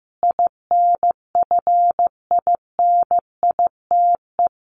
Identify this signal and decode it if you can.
Morse code, heard as a keyed tone, about 15 wpm, 700 Hz. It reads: INFINITE
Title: A Sound Of Thunder 15wpm